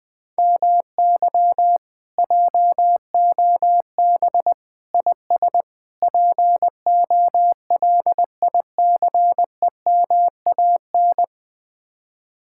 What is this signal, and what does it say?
Morse code, heard as a keyed tone, about 21 wpm, 700 Hz. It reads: MY JOB IS POLICEMAN
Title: Speed-Racing Common QSO Phrases Encore - 20wpm